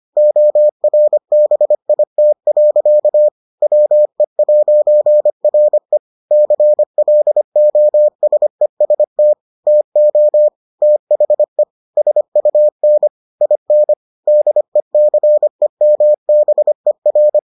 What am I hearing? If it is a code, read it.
Morse code, heard as a keyed tone, about 25 wpm, 600 Hz. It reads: ORBIT. WE'RE CLOSEST TO THE SUN IN DECEMBER